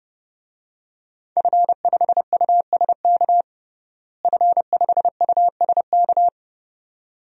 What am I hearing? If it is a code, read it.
Morse code, heard as a keyed tone, about 30 wpm, 700 Hz. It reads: F5USK F5USK
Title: DXCC-100 Call Signs - 5 Characters - Repeated 2x - 30wpm